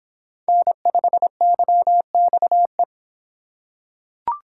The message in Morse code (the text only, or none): N5YXE